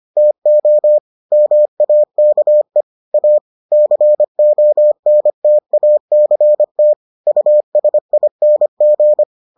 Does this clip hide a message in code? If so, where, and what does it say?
Yes: Morse code from 0.2 to 9.2 s, heard as a keyed tone, about 25 wpm, 600 Hz. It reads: TO MAKE A CONTACT USING